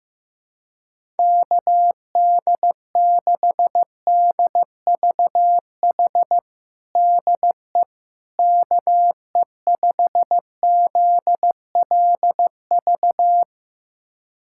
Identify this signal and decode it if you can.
Morse code, heard as a keyed tone, about 16 wpm, 700 Hz. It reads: KD6DVH DE KE5ZLV
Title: USA Callsign DE Callsign 15wpm